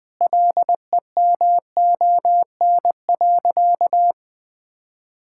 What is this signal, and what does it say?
Morse code, heard as a keyed tone, about 21 wpm, 700 Hz. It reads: LEMON.
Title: Sentences with 5-Word Phrases - 20wpm